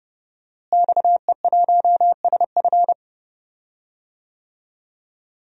Morse code, heard as a keyed tone, about 32 wpm, 700 Hz. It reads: XE1SF
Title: DXCC-100 Call Signs - 5 Characters - Repeated 3x - 30wpm